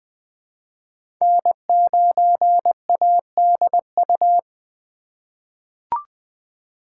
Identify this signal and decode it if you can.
Morse code, heard as a keyed tone, about 20 wpm, 700 Hz. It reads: N9ADU
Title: USA Call Signs - 5 Characters - 2x Character Spacing - 20wpm